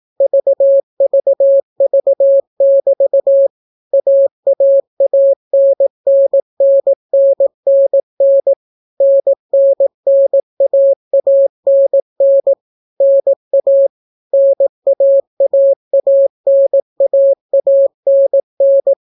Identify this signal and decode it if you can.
Morse code, heard as a keyed tone, about 18 wpm, 550 Hz. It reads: VVV<BT> AAANNNNNN NNNAANN NA NAAANAANN